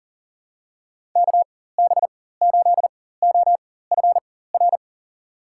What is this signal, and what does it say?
Morse code, heard as a keyed tone, about 40 wpm, spaced as for 19 wpm, 700 Hz. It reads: KB8OFR
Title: USA Call Signs - 6 Characters - 4x Character Spacing - 40wpm